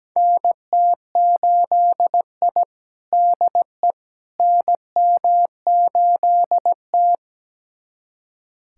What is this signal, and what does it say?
Morse code, heard as a keyed tone, about 17 wpm, 700 Hz. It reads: NT8I DE NM8T
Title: USA Callsign DE Callsign - 4 Characters - 17wpm